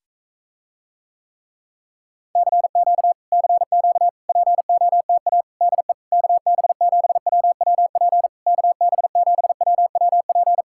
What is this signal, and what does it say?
Morse code, heard as a keyed tone, about 45 wpm, 700 Hz. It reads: CQ CQ POTA DE KB7WWP KB7WWP